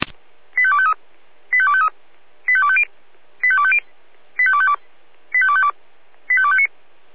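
Five-tone selective call (ZVEI), heard as quick runs of five tones, tones of about 70 ms. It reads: 85253 85253 85250 85250 85252 85252 85259
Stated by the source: ZVEI five-tone selective call received on an SDR.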